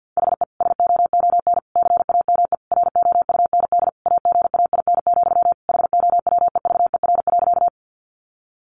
Morse code, heard as a keyed tone, about 50 wpm, 700 Hz. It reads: HE SOON CAME ROUND AGAIN, HOWEVER.
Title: Sentences from Top 600 Words 50wpm